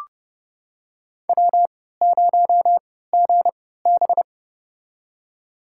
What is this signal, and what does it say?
Morse code, heard as a keyed tone, about 30 wpm, spaced as for 17 wpm, 700 Hz. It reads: W0GB
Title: USA Call Signs - 4 Characters - 3x Character Spacing - 30wpm